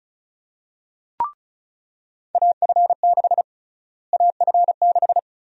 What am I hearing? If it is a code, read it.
Morse code, heard as a keyed tone, about 34 wpm, 700 Hz. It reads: AF6 AF6
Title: USA Call Sign Prefixes - Repeated 2x - 35wpm